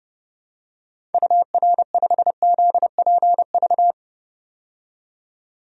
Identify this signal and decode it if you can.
Morse code, heard as a keyed tone, about 30 wpm, 700 Hz. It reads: UR5ZPV
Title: DXCC-100 Call Signs - 6 Characters - 30wpm